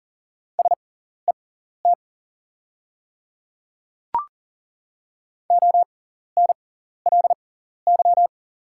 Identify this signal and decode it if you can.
Morse code, heard as a keyed tone, about 40 wpm, spaced as for 14 wpm, 690 Hz. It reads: SET ONLY
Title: 6x Character Spacing - Top 200 Words - 40wpm